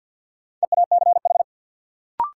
Morse code, heard as a keyed tone, about 53 wpm, 700 Hz. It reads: EACH
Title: Sets of 2 Words - New Word Drill - 50wpm